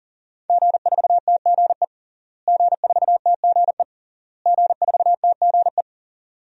Morse code, heard as a keyed tone, about 40 wpm, 700 Hz. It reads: G4TGE G4TGE G4TGE
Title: DXCC-100 Call Signs - 5 Characters - Repeated 3x - 40wpm